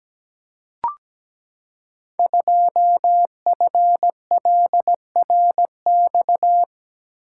Morse code, heard as a keyed tone, about 17 wpm, 700 Hz. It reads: 2FLRX
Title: Code Groups - 5 Letter:Numbers - 17wpm